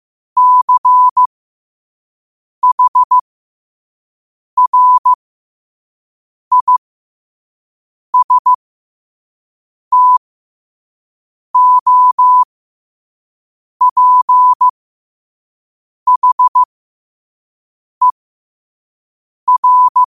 Morse code, heard as a keyed tone, about 15 wpm, spaced as for 5 wpm, 1000 Hz. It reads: CHRISTOPHER